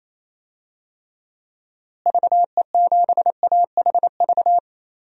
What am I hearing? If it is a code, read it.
Morse code, heard as a keyed tone, about 28 wpm, 700 Hz. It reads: VE7AHV
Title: DXCC-100 Call Signs - 6 Characters - Repeated 2x - 28wpm